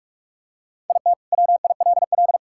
Morse code, heard as a keyed tone, about 41 wpm, 700 Hz. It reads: IT WILL
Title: Sentences from Top 1600 Words 45wpm